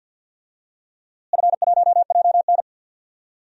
Morse code, heard as a keyed tone, about 50 wpm, 700 Hz. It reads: F1JN
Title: DXCC-100 Call Signs - 4 Characters - Repeated 3x - 50wpm